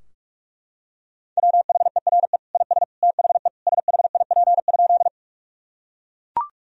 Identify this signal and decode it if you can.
Morse code, heard as a keyed tone, about 41 wpm, 700 Hz. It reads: WHERE IS THE SHIP?
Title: Sentences from Top 1000 Words - Review Part 2 - 45wpm